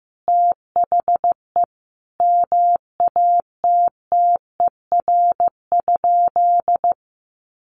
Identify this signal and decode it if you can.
Morse code, heard as a keyed tone, about 15 wpm, 700 Hz. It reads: THE MATTER?
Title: Sentences from Top 1100 Words 15wpm